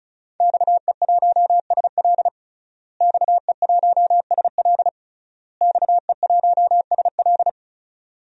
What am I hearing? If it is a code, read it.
Morse code, heard as a keyed tone, about 36 wpm, 700 Hz. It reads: XE1SL XE1SL XE1SL
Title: DXCC-100 Call Signs - 5 Characters - Repeated 3x - 35wpm